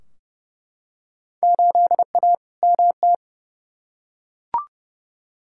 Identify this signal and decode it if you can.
Morse code, heard as a keyed tone, about 32 wpm, 700 Hz. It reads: 8A MT
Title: Field Day Exchanges Encore - 30wpm